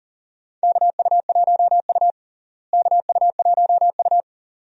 Morse code, heard as a keyed tone, about 40 wpm, 700 Hz. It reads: KU1U KU1U
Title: USA Call Signs - 4 Characters - Repeated 2x - 40wpm